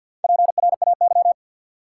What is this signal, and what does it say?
Morse code, heard as a keyed tone, about 51 wpm, 700 Hz. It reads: PRAY
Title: Four Letter Words - 50wpm